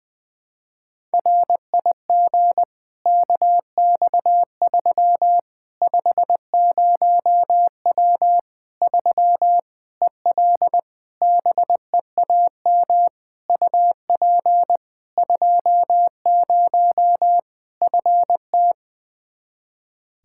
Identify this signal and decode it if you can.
Morse code, heard as a keyed tone, about 21 wpm, 700 Hz. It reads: RIG KX3 50W 3 EL BEAM UP 20 FT
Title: Practice QSO - Exchange 2 - Info - 20wpm